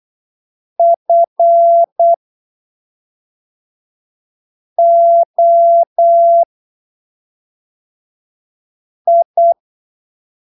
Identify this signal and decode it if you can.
Morse code, heard as a keyed tone, about 8 wpm, spaced as for 3 wpm, 680 Hz. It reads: FOI
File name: arquivos / texto / texto 16 08ppm